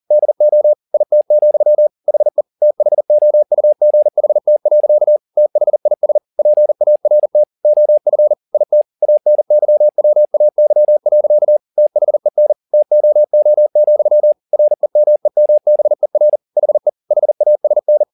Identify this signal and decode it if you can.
Morse code, heard as a keyed tone, about 40 wpm, 600 Hz. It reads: DO IT, HE THOUGHT. THIS PART OF IT ANYWAY. THEN TOO, REMEMBER HE HASN
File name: the old man and the sea 2 40wpm